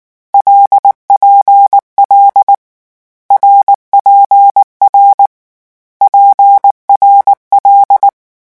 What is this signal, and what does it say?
Morse code, heard as a keyed tone, about 19 wpm, 800 Hz. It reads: LPL RPR PRL
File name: morse mp3 / lecon6-vitesse 18